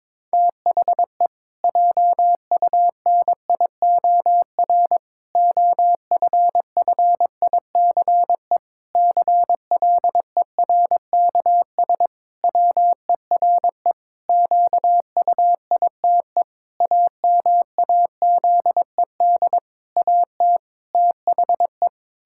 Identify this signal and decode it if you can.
Morse code, heard as a keyed tone, about 22 wpm, 700 Hz. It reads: THE JUNIOR OFFICE CLERKS WERE QUITE AMAZED AT THE
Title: Pangram Challenge v2 - 22wpm